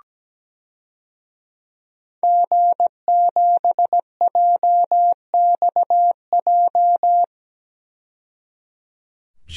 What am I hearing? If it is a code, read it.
Morse code, heard as a keyed tone, about 17 wpm, 700 Hz. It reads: G7JXJ
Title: DXCC-100 Call Signs - 5 Characters - 17wpm